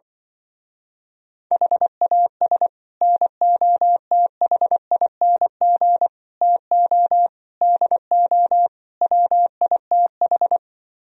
Morse code, heard as a keyed tone, about 24 wpm, 700 Hz. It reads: HAS NOTHING TO DO WITH
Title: Speed-Racing - Sets of 5 Words 20wpm